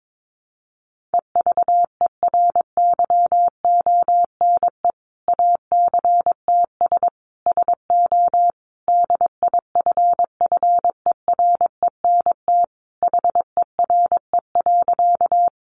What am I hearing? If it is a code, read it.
Morse code, heard as a keyed tone, about 22 wpm, 700 Hz. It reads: EVERYONE ACTS SO DIFFERENT HERE.
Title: Easy Phrases 22wpm